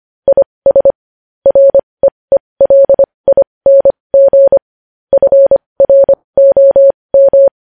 Morse code, heard as a keyed tone, about 25 wpm, spaced as for 20 wpm, 550 Hz. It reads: IS REELING FROM
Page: News Headlines in Morse Code at 20 WPM